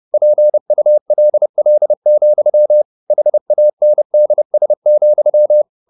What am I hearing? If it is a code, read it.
Morse code, heard as a keyed tone, about 30 wpm, 600 Hz. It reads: PULL, HANDS,